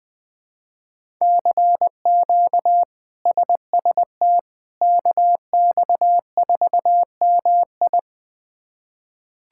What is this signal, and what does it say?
Morse code, heard as a keyed tone, about 21 wpm, 700 Hz. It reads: CQ SST KX4MI
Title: K1USN SST QSO Practice - 20wpm